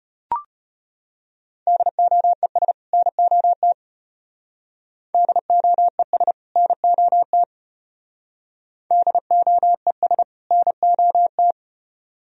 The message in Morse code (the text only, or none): DOES NOT DOES NOT DOES NOT